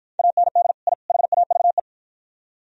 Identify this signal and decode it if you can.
Morse code, heard as a keyed tone, about 54 wpm, 700 Hz. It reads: AND I HAVE